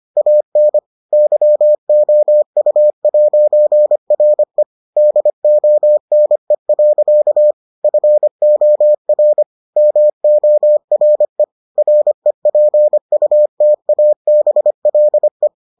Morse code, heard as a keyed tone, about 25 wpm, 600 Hz. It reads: AN YOU'RE DONE. FOR MORE REPUTABLE